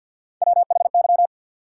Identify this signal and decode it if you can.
Morse code, heard as a keyed tone, about 47 wpm, 700 Hz. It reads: WHY
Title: Sets of 3 Words - New Word Drill - 50wpm